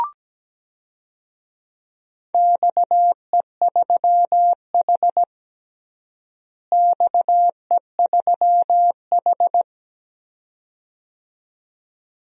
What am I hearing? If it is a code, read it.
Morse code, heard as a keyed tone, about 17 wpm, 700 Hz. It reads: XE3H XE3H
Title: DXCC-100 Call Signs - 4 Characters - Repeated 2x - 17wpm